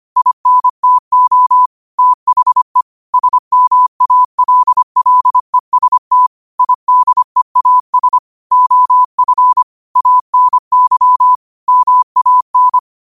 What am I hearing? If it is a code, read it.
Morse code, heard as a keyed tone, about 25 wpm, 1000 Hz. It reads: INTO THE SMALLEST IDEAS OF ANY MAN
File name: qotdfiles / 25wpm